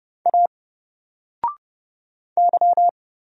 Morse code, heard as a keyed tone, about 30 wpm, 700 Hz. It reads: A Y
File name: Single-Letter-Number-30wpm